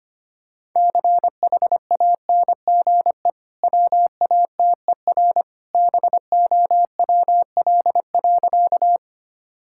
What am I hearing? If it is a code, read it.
Morse code, heard as a keyed tone, about 25 wpm, 700 Hz. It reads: CHANGE WATER BOWL.
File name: Easy Sentences with Hints - 25wpm